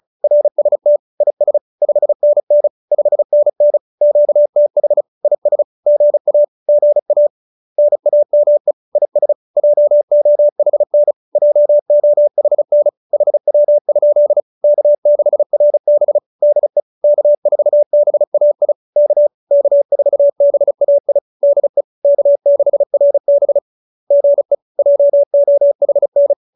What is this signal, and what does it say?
Morse code, heard as a keyed tone, about 34 wpm, 590 Hz. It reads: RST IS 5NN 5NN QTH IS GA GA NAME IS JOHN JOHN HW? K6RB DE K4BAI K K4BAI DE K6RB GE JOHN